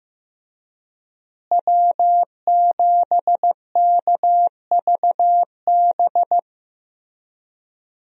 Morse code, heard as a keyed tone, about 15 wpm, 700 Hz. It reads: W7KVB
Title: USA Call Signs - 5 Characters - Repeated 3x - 15wpm